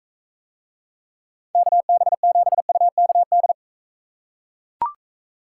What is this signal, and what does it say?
Morse code, heard as a keyed tone, about 41 wpm, 700 Hz. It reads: KB7UKD